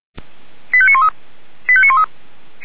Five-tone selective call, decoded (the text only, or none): 85913 85913